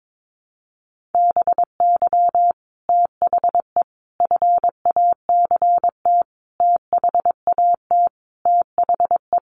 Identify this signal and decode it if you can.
Morse code, heard as a keyed tone, about 22 wpm, 700 Hz. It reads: BY THE FACT THAT THE
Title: Sets of 5 Words Encore 22wpm